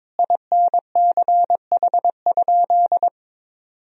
Morse code, heard as a keyed tone, about 22 wpm, 700 Hz. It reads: INCH?
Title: Sentences from Top 1000 Words - Review Part 3 - 22wpm